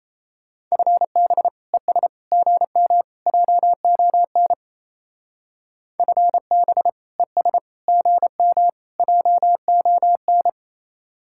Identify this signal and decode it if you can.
Morse code, heard as a keyed tone, about 34 wpm, 700 Hz. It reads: FB ES GM JON FB ES GM JON
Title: Speed-Racing Common QSO Phrases - 28wpm